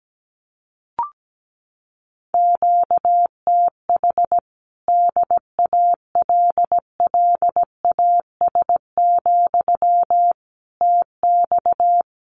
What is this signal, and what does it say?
Morse code, heard as a keyed tone, about 17 wpm, 700 Hz. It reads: QTH DALLAS, TX
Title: Common QSO Phrases Encore 17wpm